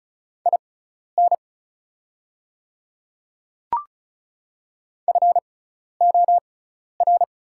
Morse code, heard as a keyed tone, about 36 wpm, spaced as for 12 wpm, 700 Hz. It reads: IN FOR
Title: 6x Character Spacing - Top 200 Words - 35wpm